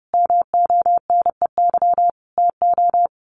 Morse code, heard as a keyed tone, about 30 wpm, 700 Hz. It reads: MONEY TO
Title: Sets of 5 Words Encore2 30wpm